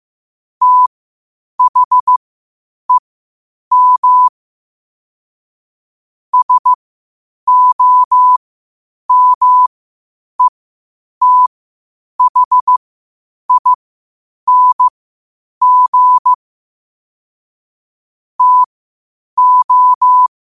Morse code, heard as a keyed tone, about 15 wpm, spaced as for 8 wpm, 1000 Hz. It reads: THEM SOMETHING TO
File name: qotdfiles / 8wpm